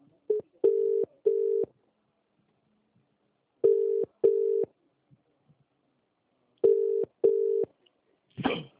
Ringback tone (UK), 400 + 450 Hz.